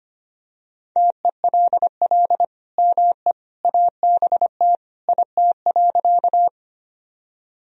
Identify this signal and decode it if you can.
Morse code, heard as a keyed tone, about 25 wpm, 700 Hz. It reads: TELL ME ABT IT.
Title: Sentences from Top 200 Words and Abbreviations 25wpm